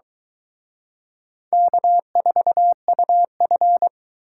Morse code, heard as a keyed tone, about 24 wpm, 700 Hz. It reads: K4UF